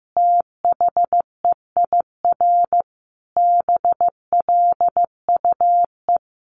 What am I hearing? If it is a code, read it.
Morse code, heard as a keyed tone, about 15 wpm, 700 Hz. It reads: THEIR BLUE
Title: A Sound Of Thunder 15wpm